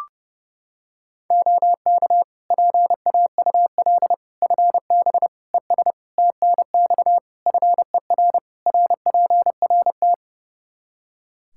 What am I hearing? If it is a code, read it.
Morse code, heard as a keyed tone, about 30 wpm, 700 Hz. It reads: OK PAUL FB ES TNX FER RPRT
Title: Practice QSO - Exchange 2 - Info - 30wpm